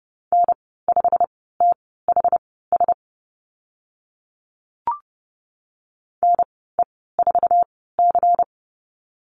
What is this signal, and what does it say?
Morse code, heard as a keyed tone, about 30 wpm, spaced as for 17 wpm, 700 Hz. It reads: N5THS NE4C
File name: Lesson 19g - Letter w - Call Signs 30wpm